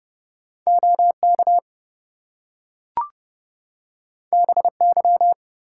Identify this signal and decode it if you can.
Morse code, heard as a keyed tone, about 29 wpm, 700 Hz. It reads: OK BY
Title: Two Letter Words - 30wpm